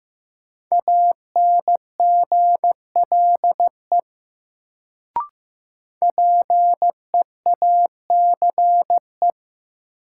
Morse code, heard as a keyed tone, about 15 wpm, 700 Hz. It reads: ANGLE PEACE
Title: Sentences from Top 1500 Words - New Word Drill Alt - 15wpm